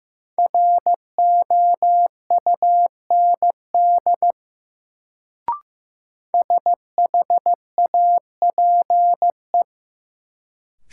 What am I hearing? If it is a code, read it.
Morse code, heard as a keyed tone, about 15 wpm, 700 Hz. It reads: ROUND SHAPE
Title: Sets of 5 Words Encore - New Word Drill - 15wpm